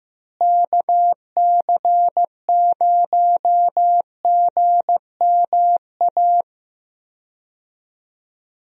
Morse code, heard as a keyed tone, about 15 wpm, 700 Hz. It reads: KC0GMA
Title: USA Call Signs - 6 Characters - Repeated 2x - 15wpm